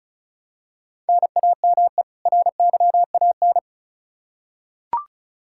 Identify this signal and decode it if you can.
Morse code, heard as a keyed tone, about 36 wpm, 700 Hz. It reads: NAME RYAN
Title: Practice QSO - Exchange 1 - Beginning - 35wpm